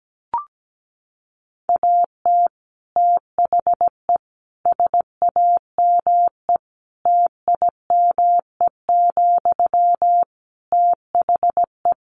Morse code, heard as a keyed tone, about 17 wpm, 700 Hz. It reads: AT THE SAME TIME, THE
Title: Sentences from Top 1500 Words 17wpm